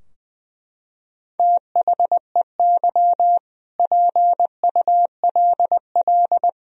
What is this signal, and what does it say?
Morse code, heard as a keyed tone, about 20 wpm, 700 Hz. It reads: THEY PULL
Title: Sentences from Top 1000 Words - Review Part 2 - 20wpm